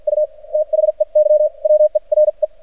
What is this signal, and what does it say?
Morse code, heard as a keyed tone, about 53 wpm, spaced as for 42 wpm, 610 Hz. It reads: U THEY WERE
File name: archived / CRSnet